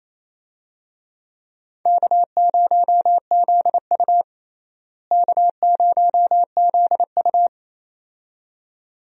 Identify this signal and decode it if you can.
Morse code, heard as a keyed tone, about 28 wpm, 700 Hz. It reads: K0ZU K0ZU
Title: USA Call Signs - 4 Characters - Repeated 2x - 28wpm